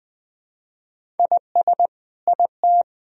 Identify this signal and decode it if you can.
Morse code, heard as a keyed tone, about 21 wpm, 700 Hz. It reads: IS IT
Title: Sentences from Top 1000 Words - Review Part 4 - 20wpm